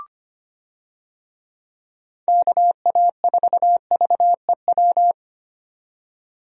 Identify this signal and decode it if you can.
Morse code, heard as a keyed tone, about 25 wpm, 700 Hz. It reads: KA4VEW
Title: USA Call Signs - 6 Characters - 25wpm